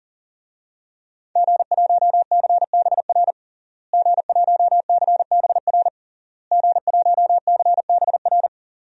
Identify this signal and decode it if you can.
Morse code, heard as a keyed tone, about 37 wpm, 700 Hz. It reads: G1CBR G1CBR G1CBR
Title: DXCC-100 Call Signs - 5 Characters - Repeated 3x - 40wpm